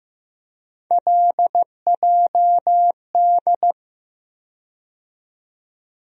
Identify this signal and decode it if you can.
Morse code, heard as a keyed tone, about 15 wpm, 700 Hz. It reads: LJD
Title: USA Call Sign Postfixes - 3 Letters 15wpm